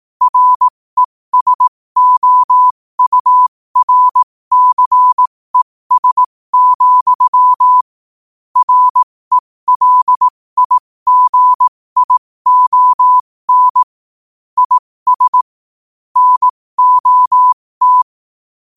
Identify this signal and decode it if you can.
Morse code, heard as a keyed tone, about 18 wpm, spaced as for 15 wpm, 1000 Hz. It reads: RESOURCES, RELIGION IS NOT
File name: qotdfiles / 15wpm